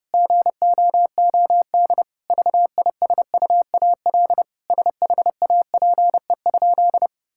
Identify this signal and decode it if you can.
Morse code, heard as a keyed tone, about 30 wpm, 700 Hz. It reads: GOOD VISUAL SHAPE?
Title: Sentences from Top 1000 Words - Review Part 6 - 30wpm